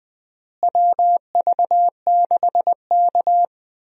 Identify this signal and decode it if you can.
Morse code, heard as a keyed tone, about 20 wpm, 700 Hz. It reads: WV6K
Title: USA Call Signs - 4 Characters - Repeated 2x - 20wpm